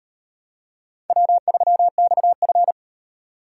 Morse code, heard as a keyed tone, about 38 wpm, 700 Hz. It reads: W3XF